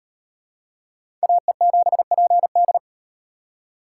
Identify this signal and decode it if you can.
Morse code, heard as a keyed tone, about 36 wpm, 700 Hz. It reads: AE7PD